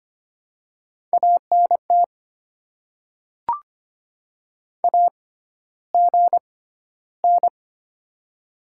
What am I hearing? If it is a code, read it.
Morse code, heard as a keyed tone, about 25 wpm, spaced as for 13 wpm, 700 Hz. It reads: ANT AGN
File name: Top 20 QSO Elements - 6x Character Spacing - 25wpm